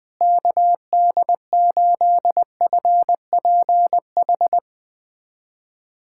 Morse code, heard as a keyed tone, about 20 wpm, 700 Hz. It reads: KD8FPH
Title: USA Callsign DE Callsign 20wpm